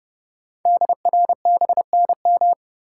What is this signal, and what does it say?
Morse code, heard as a keyed tone, about 30 wpm, 700 Hz. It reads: DRBNM
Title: Code Groups - 5 Letters - 30wpm